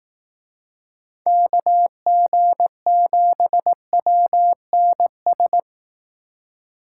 Morse code, heard as a keyed tone, about 18 wpm, 700 Hz. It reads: KG7WNS
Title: Speed Racing - USA 2x3 Call Signs 15wpm